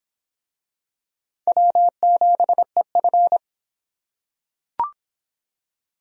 Morse code, heard as a keyed tone, about 24 wpm, 700 Hz. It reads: W7EF